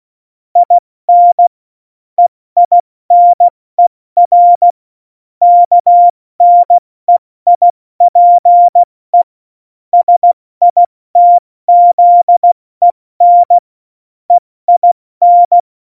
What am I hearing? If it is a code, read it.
Morse code, heard as a keyed tone, about 16 wpm, 700 Hz. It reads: IN EINER KNEIPE SITZEN EIN
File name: Morsekurs / Witze / Witz 08. Oktober 2019, 70 F80